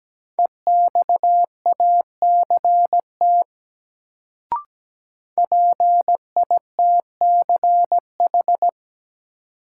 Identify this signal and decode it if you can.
Morse code, heard as a keyed tone, about 17 wpm, 700 Hz. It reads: EXACT PITCH
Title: Sentences from Top 1300 Words - New Word Drill Alt - 17wpm